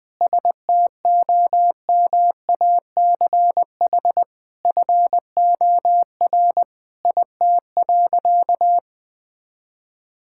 Morse code, heard as a keyed tone, about 20 wpm, 700 Hz. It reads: STOMACH FOR IT.
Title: All-Age Appropriate Jokes - 20wpm